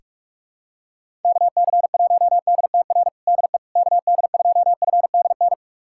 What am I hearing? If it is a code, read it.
Morse code, heard as a keyed tone, about 46 wpm, 700 Hz. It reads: KC1DTR DE KD2FDN
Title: USA Callsign DE Callsign 45wpm